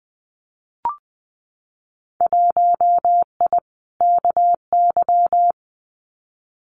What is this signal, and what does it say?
Morse code, heard as a keyed tone, about 20 wpm, 700 Hz. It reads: 1I KY
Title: Winter Field Day Exchanges 20wpm